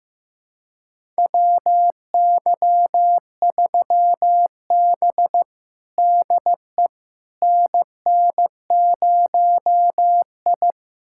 Morse code, heard as a keyed tone, about 15 wpm, 700 Hz. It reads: WY3B DE NN0I
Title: USA Callsign DE Callsign - 4 Characters - 15wpm